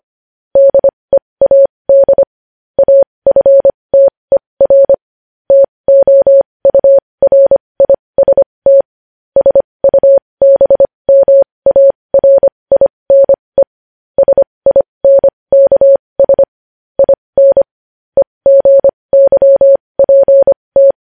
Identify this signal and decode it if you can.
Morse code, heard as a keyed tone, about 25 wpm, spaced as for 20 wpm, 550 Hz. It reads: DEAD AFTER TOURIST SUBMARINE SINKS IN EGYPT
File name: podcast / News headlines-1x-25x20